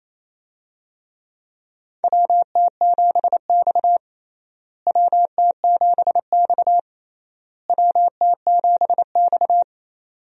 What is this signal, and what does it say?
Morse code, heard as a keyed tone, about 28 wpm, 700 Hz. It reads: WT7X WT7X WT7X
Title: USA Call Signs - 4 Characters - Repeated 3x - 28wpm